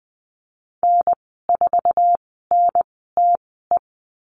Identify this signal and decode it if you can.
Morse code, heard as a keyed tone, about 20 wpm, spaced as for 15 wpm, 700 Hz. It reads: N4NTE
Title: 088 Lesson 15g - Number 5 - Call Signs 20wpm